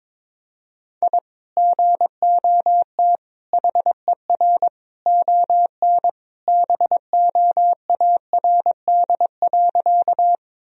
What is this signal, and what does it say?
Morse code, heard as a keyed tone, about 22 wpm, 700 Hz. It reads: I GOT HER ON BOARD.
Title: Sentences from Top 1000 Words - Review Part 4 - 22wpm